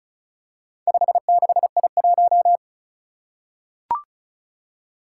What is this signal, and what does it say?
Morse code, heard as a keyed tone, about 36 wpm, 700 Hz. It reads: 56I1
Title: Four Challenging Letter-Number Combinations 35wpm